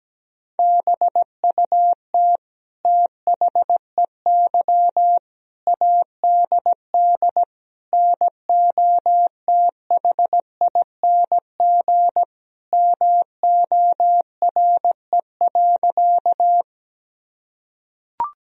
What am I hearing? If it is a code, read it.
Morse code, heard as a keyed tone, about 17 wpm, 700 Hz. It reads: BUT THEY ADD NOTHING MORE.
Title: Sentences from Top 1000 Words - Review Part 1 - 17wpm